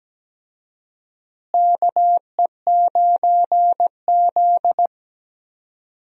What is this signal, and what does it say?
Morse code, heard as a keyed tone, about 17 wpm, 700 Hz. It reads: KE9Z
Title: USA Call Signs - 4 Characters - Repeated 2x - 17wpm